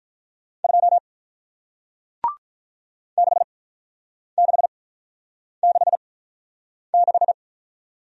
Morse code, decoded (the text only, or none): 2 6666